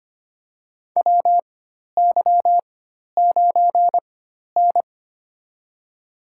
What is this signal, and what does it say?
Morse code, heard as a keyed tone, about 25 wpm, spaced as for 12 wpm, 700 Hz. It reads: WY9N
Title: USA Call Signs - 4 Characters - 4x Character Spacing - 25wpm